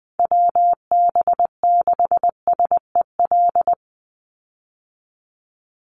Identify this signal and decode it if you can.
Morse code, heard as a keyed tone, about 20 wpm, 700 Hz. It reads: WB6SEL
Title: Call Signs Repeated 3x - 20wpm